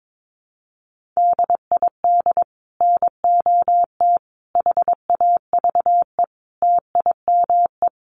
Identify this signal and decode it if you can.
Morse code, heard as a keyed tone, about 22 wpm, 700 Hz. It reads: DID NOT HAVE TIME
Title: Sets of 5 Words Encore 22wpm